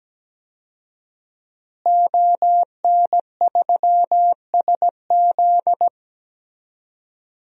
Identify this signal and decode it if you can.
Morse code, heard as a keyed tone, about 17 wpm, 700 Hz. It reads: ON3SZ
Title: DXCC-100 Call Signs - 5 Characters - Repeated 3x - 17wpm